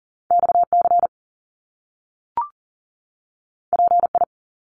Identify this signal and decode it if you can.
Morse code, heard as a keyed tone, about 40 wpm, 700 Hz. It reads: XC PI
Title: Top Two Letter Combinations in Words 40wpm